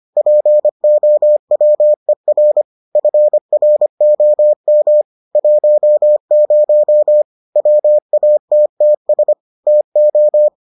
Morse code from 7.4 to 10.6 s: WATTS TO

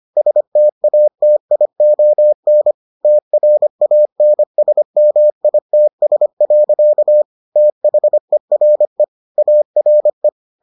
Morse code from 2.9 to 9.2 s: TRANSMITS. THERE

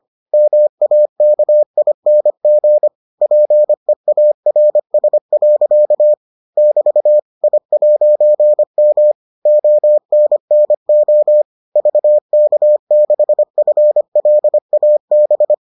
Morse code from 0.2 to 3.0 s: MAKING

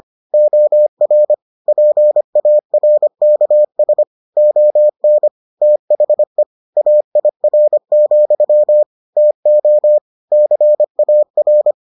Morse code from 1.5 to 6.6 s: PARKS ON THE

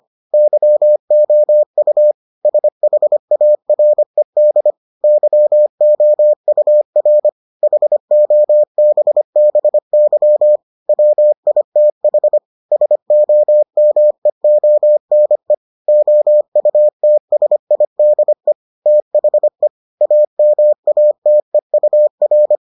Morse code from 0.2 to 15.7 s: YOU SHARED YOUR HOBBY WITH SOMEONE

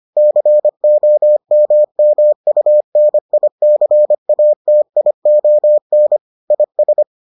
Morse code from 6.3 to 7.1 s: IS

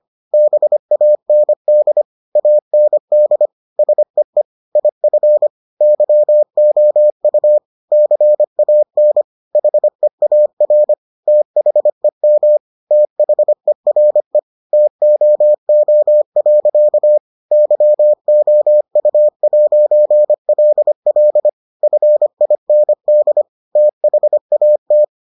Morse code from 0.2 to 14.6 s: BAND AND SEE IF YOU CAN HEAR THEM THERE

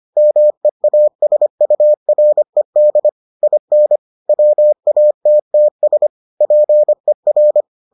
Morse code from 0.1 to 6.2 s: MEASURED IN WATTS

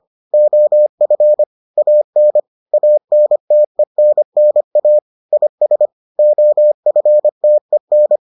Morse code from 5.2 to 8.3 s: IS OFTEN

